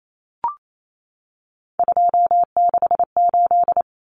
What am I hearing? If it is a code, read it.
Morse code, heard as a keyed tone, about 28 wpm, 700 Hz. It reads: 268
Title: Common Cut Numbers - 3-Digit 28wpm